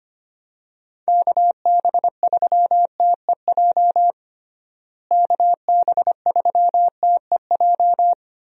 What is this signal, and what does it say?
Morse code, heard as a keyed tone, about 25 wpm, 700 Hz. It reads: KB3TEJ KB3TEJ
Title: USA Call Signs - 6 Characters - Repeated 2x - 25wpm